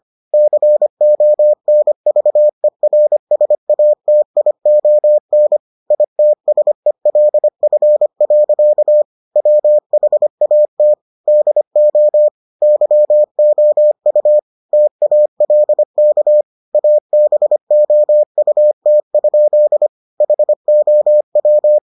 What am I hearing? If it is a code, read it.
Morse code, heard as a keyed tone, about 25 wpm, 600 Hz. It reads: CONVERSATION ITSELF. WHAT DO YOU TALK ABOUT? HOW